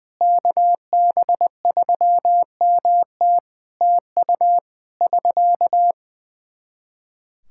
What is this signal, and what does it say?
Morse code, heard as a keyed tone, about 20 wpm, 700 Hz. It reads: KB3MT TU <SK>
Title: Practice QSO - Exchange 4 - Ending - 20wpm